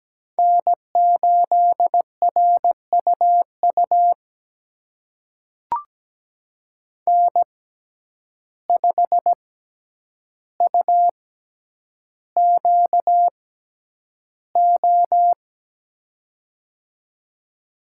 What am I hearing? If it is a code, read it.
Morse code, heard as a keyed tone, about 17 wpm, spaced as for 9 wpm, 700 Hz. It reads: N8RUU N5UQO
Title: USA Call Signs - 5 Characters - 6x Character Spacing - 17wpm